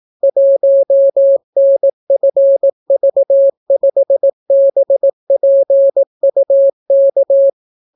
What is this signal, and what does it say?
Morse code, heard as a keyed tone, about 18 wpm, 550 Hz. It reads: 1NFV5BPUK